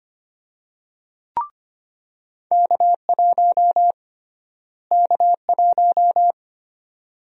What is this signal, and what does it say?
Morse code, heard as a keyed tone, about 25 wpm, 700 Hz. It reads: K1 K1
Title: USA Call Sign Prefixes - Repeated 2x - 25wpm